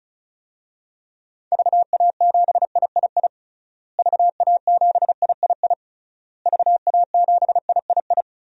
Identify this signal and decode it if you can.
Morse code, heard as a keyed tone, about 34 wpm, 700 Hz. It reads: VA7III VA7III VA7III
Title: DXCC-100 Call Signs - 6 Characters - Repeated 3x - 35wpm